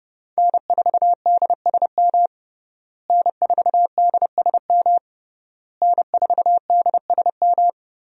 Morse code, heard as a keyed tone, about 30 wpm, 700 Hz. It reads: N4DSM N4DSM N4DSM
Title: USA Call Signs - 5 Characters - Repeated 3x - 30wpm